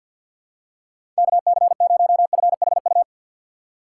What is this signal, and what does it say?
Morse code, heard as a keyed tone, about 50 wpm, 700 Hz. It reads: KC0FHU
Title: USA Call Signs - 6 Characters - 3x Character Spacing - 50wpm